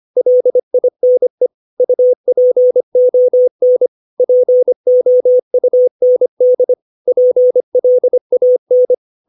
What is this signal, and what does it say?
Morse code, heard as a keyed tone, about 25 wpm, 500 Hz. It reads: LINE UPON POUND PLAN
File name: 632 words / 632 Words 25WPM